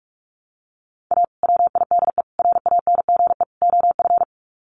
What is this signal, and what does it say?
Morse code, heard as a keyed tone, about 44 wpm, 700 Hz. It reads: A WIDE RANGE OF
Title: Sets of 4 Words 45wpm